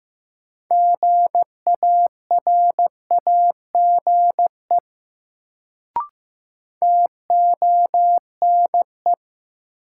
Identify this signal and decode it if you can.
Morse code, heard as a keyed tone, about 15 wpm, 700 Hz. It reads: GARAGE TONE
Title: Sentences from Top 1600 Words - New Word Drill - 15wpm